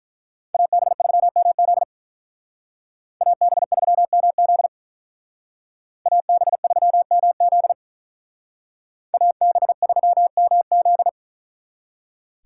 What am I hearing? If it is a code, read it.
Morse code, heard as a keyed tone, about 43 wpm, 700 Hz. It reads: AB3MZ AB3MZ AB3MZ AB3MZ
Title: Speed Racing - USA Call Signs 35wpm